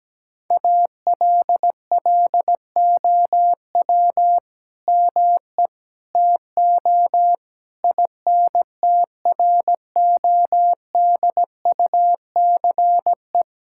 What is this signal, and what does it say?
Morse code, heard as a keyed tone, about 17 wpm, 700 Hz. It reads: ALLOW ME TO INTRODUCE
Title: Sentences from Top 1600 Words 17wpm